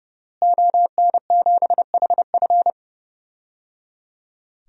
Morse code, heard as a keyed tone, about 30 wpm, 700 Hz. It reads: ON7HF
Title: DXCC-100 Call Signs - 5 Characters - Repeated 3x - 30wpm